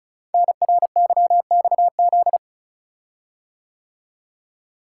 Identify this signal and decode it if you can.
Morse code, heard as a keyed tone, about 36 wpm, 700 Hz. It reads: NRYXZ
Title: Code Groups - 5 Letters - 35wpm